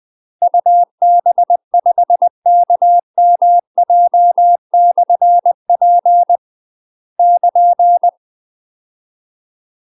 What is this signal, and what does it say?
Morse code, heard as a keyed tone, about 20 wpm, 700 Hz. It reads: UB5KMJ/P <KN>
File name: UK MASTER QSOs TRAINING / 20WPM / UK MASTER QSOs 0024